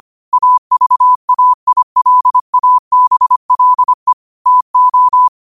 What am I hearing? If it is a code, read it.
Morse code, heard as a keyed tone, about 25 wpm, 1000 Hz. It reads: AVAILABLE TO
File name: qotdfiles / 25wpm